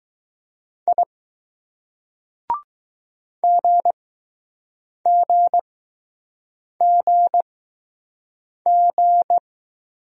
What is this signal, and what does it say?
Morse code, heard as a keyed tone, about 20 wpm, 700 Hz. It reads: I G G G G